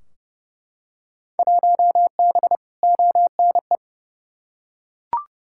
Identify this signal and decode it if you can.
Morse code, heard as a keyed tone, about 30 wpm, 700 Hz. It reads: 1B ONE
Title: Field Day Exchanges - 30wpm